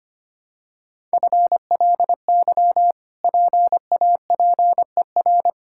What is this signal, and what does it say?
Morse code, heard as a keyed tone, about 25 wpm, 700 Hz. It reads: FLY PAPER
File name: Easy Sentences with Hints - 25wpm